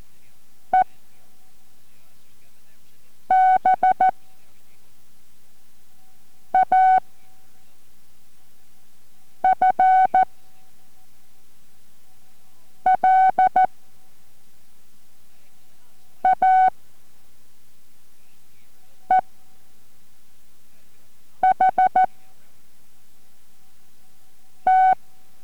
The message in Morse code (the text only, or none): E B A F L A E H T